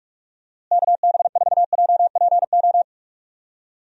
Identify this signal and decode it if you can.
Morse code, heard as a keyed tone, about 46 wpm, 700 Hz. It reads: KB4JPO
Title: USA Call Signs - 6 Characters - Repeated 3x - 45wpm